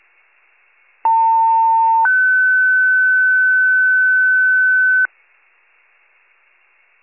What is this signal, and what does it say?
Two-tone page, heard as a one-second tone, then a three-second tone, about 900 then 1500 Hz. It is heard from 1.0 to 5.1 s.